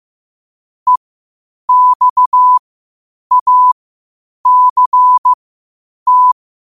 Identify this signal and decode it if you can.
Morse code, heard as a keyed tone, about 15 wpm, spaced as for 8 wpm, 1000 Hz. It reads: EXACT